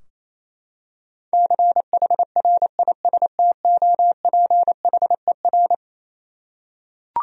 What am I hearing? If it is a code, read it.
Morse code, heard as a keyed tone, about 28 wpm, 700 Hz. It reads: CHRISTOPHER
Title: USA Common Names - 28wpm